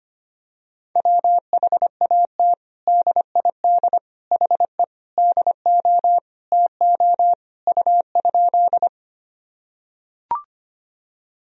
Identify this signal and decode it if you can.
Morse code, heard as a keyed tone, about 25 wpm, 700 Hz. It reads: WHAT DID HE DO TO U?
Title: Sentences from Top 100 Words and Abbreviations 25wpm